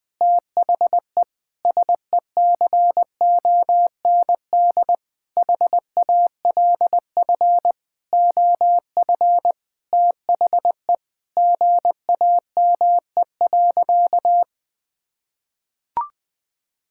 Morse code, heard as a keyed tone, about 20 wpm, 700 Hz. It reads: THE SECOND HALF OF THE GAME.
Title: Sentences with 6-Word Phrases - Spoken Beginning - 20wpm